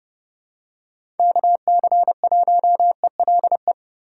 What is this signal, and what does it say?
Morse code, heard as a keyed tone, about 31 wpm, 700 Hz. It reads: KC1ELE
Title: USA Call Signs - 6 Characters - 2x Character Spacing - 30wpm